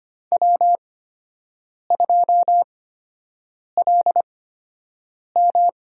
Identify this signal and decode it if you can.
Morse code, heard as a keyed tone, about 25 wpm, spaced as for 7 wpm, 700 Hz. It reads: W2LM
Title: USA Call Signs - 5 Characters - 8x Character Spacing - 25wpm